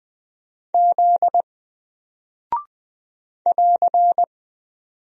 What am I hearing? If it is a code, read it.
Morse code, heard as a keyed tone, about 20 wpm, 700 Hz. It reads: Z <AR>
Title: Single Character (Letter:Number:Punctuation:Prosign) 20wpm